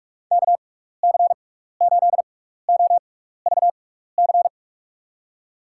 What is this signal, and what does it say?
Morse code, heard as a keyed tone, about 42 wpm, spaced as for 15 wpm, 700 Hz. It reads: KC8OVC
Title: USA Call Signs - 6 Characters - 6x Character Spacing - 45wpm